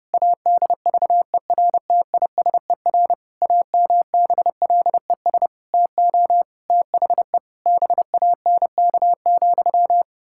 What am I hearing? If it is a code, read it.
Morse code, heard as a keyed tone, about 32 wpm, 700 Hz. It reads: ADVERTISER AMBLES TO THE BANK,
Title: Pangram Challenge v2 - 30wpm